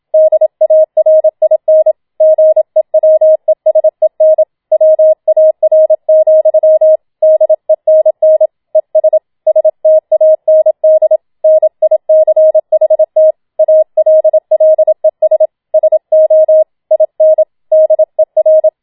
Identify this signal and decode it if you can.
Morse code, heard as a keyed tone, about 27 wpm, 630 Hz. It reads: DARIN GEWESEN WAR, DENN ES STAND NICHT ALLES SO IN DER
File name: Schneewittchen Teil2 27wpm